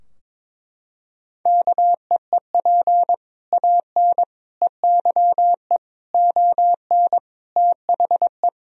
Morse code, heard as a keyed tone, about 21 wpm, 700 Hz. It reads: KEEP AN EYE ON THE
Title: Sentences from Top 1000 Words - Review Part 3 - 22wpm